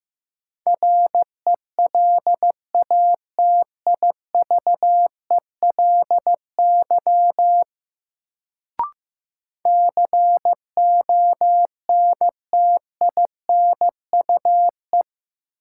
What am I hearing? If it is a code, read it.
Morse code, heard as a keyed tone, about 15 wpm, 700 Hz. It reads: RELATIVELY CONTINUE
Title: Sentences from Top 1100 Words - New Word Drill Alt - 15wpm